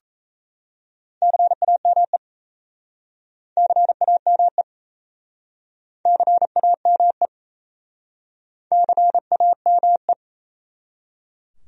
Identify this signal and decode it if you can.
Morse code, heard as a keyed tone, about 37 wpm, 700 Hz. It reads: CAME CAME CAME CAME